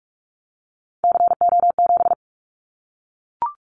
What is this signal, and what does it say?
Morse code, heard as a keyed tone, about 44 wpm, 700 Hz. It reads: CO7
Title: Callsign Prefixes for Top 100 DXCC - 45wpm